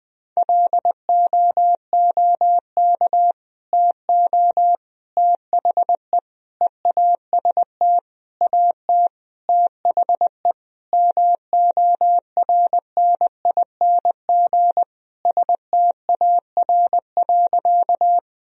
Morse code, heard as a keyed tone, about 20 wpm, 700 Hz. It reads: LOOK TO THE EAST AT THE MORNING STAR.
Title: Sentences from Top 1000 Words - Review Part 3 - 20wpm